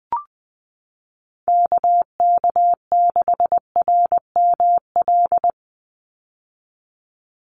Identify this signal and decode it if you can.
Morse code, heard as a keyed tone, about 19 wpm, 700 Hz. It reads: KK6RML
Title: USA Call Signs 20wpm